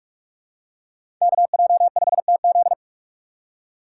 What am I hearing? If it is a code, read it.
Morse code, heard as a keyed tone, about 46 wpm, 700 Hz. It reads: KJ5TZ
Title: USA Call Signs - 5 Characters - 3x Character Spacing - 45wpm